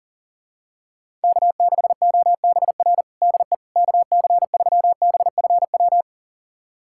Morse code, heard as a keyed tone, about 40 wpm, 700 Hz. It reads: K6OBR DE KC3BFW
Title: USA Callsign DE Callsign 40wpm